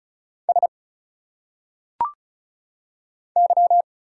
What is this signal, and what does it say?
Morse code, heard as a keyed tone, about 34 wpm, 700 Hz. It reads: S Y
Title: Single Letters - Phonetic Pronunciation 35wpm